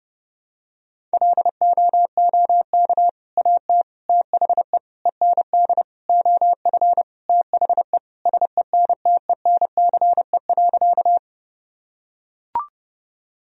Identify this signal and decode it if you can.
Morse code, heard as a keyed tone, about 30 wpm, 700 Hz. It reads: LOOK AT THE END OF THE SENTENCE.
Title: Sentences from Top 1000 Words - Review Part 1 - 30wpm